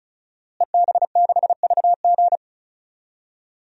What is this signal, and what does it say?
Morse code, heard as a keyed tone, about 36 wpm, 700 Hz. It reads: EB6VG
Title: DXCC-100 Call Signs - 5 Characters - Repeated 2x - 35wpm